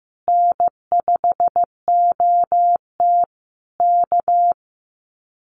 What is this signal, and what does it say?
Morse code, heard as a keyed tone, about 15 wpm, 700 Hz. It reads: N5OT K
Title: Common QSO Phrases Encore 15wpm